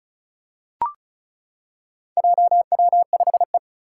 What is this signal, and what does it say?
Morse code, heard as a keyed tone, about 36 wpm, 700 Hz. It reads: JW5E
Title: Four Challenging Letter-Number Combinations 35wpm